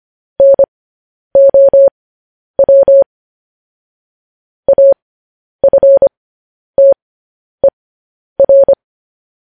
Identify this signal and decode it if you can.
Morse code, heard as a keyed tone, about 25 wpm, spaced as for 10 wpm, 550 Hz. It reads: NOW AFTER